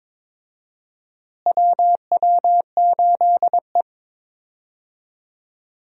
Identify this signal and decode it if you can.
Morse code, heard as a keyed tone, about 22 wpm, 700 Hz. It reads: WW8E
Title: USA Call Signs - 4 Characters - Repeated 3x - 22wpm